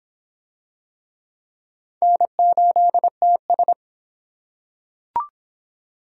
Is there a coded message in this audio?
Yes: Morse code from 2.0 to 3.7 s, heard as a keyed tone, about 24 wpm, 700 Hz. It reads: N8TS